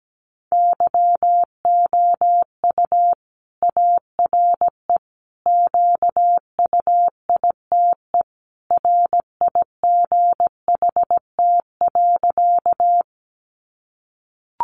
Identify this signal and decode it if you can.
Morse code, heard as a keyed tone, about 17 wpm, 700 Hz. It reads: YOU ARE QUITE RIGHT.
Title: Sentences from Top 700 Words 17wpm